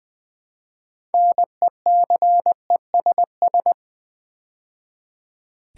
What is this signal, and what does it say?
Morse code, heard as a keyed tone, about 20 wpm, 700 Hz. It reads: NECESS
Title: Build A Word - 3 Letters At A Time - 9-Letter Words - 20wpm